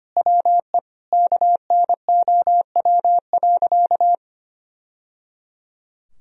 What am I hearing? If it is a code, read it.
Morse code, heard as a keyed tone, about 25 wpm, 700 Hz. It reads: WE KNOW.
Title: Sentences from Top 1000 Words - Review Part 3 - 25wpm